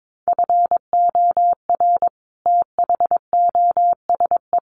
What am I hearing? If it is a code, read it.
Morse code, heard as a keyed tone, about 22 wpm, 700 Hz. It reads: FOR THOSE
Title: Sets of 5 Words Encore 22wpm